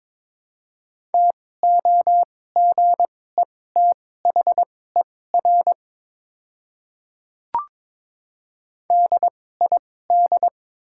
Morse code, heard as a keyed tone, about 22 wpm, spaced as for 16 wpm, 700 Hz. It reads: TOGETHER DID
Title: 2x Character Spacing - Top 300 Words - 22wpm